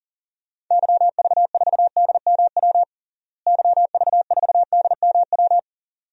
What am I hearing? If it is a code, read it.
Morse code, heard as a keyed tone, about 40 wpm, 700 Hz. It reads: YV4DMW YV4DMW
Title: DXCC-100 Call Signs - 6 Characters - Repeated 2x - 40wpm